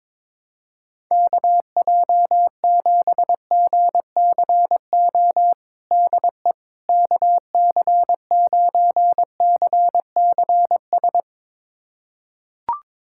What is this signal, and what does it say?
Morse code, heard as a keyed tone, about 22 wpm, 700 Hz. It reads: KJ7GCO DE KC9CCS
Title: USA Callsign DE Callsign 22wpm